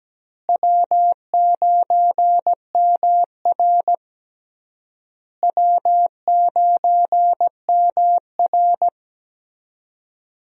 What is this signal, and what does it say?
Morse code, heard as a keyed tone, about 17 wpm, 700 Hz. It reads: W9MR W9MR
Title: USA Call Signs - 4 Characters - Repeated 3x - 17wpm